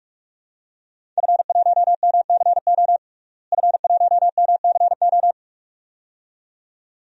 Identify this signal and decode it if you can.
Morse code, heard as a keyed tone, about 41 wpm, 700 Hz. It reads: F1MCO F1MCO
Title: DXCC-100 Call Signs - 5 Characters - Repeated 2x - 45wpm